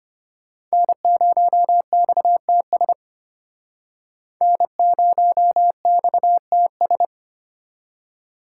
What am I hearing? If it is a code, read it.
Morse code, heard as a keyed tone, about 35 wpm, 700 Hz. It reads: N0XTS N0XTS